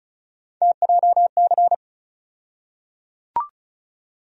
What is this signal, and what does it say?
Morse code, heard as a keyed tone, about 36 wpm, 700 Hz. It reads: TJC
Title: USA Call Sign Postfixes - 3 Letters 35wpm